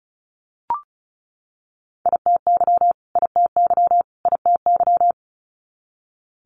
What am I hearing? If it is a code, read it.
Morse code, heard as a keyed tone, about 36 wpm, 700 Hz. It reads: ITY ITY ITY
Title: Multi-letter Phonemes Repeated 3x 35wpm